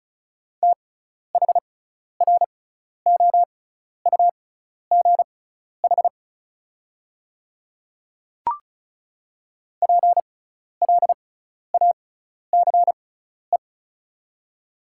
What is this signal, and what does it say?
Morse code, heard as a keyed tone, about 36 wpm, spaced as for 12 wpm, 700 Hz. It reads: THROUGH PLACE
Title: 6x Character Spacing - Top 200 Words - 35wpm